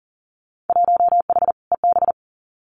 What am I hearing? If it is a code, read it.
Morse code, heard as a keyed tone, about 40 wpm, 700 Hz. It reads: 1H EB
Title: Winter Field Day Exchanges 40wpm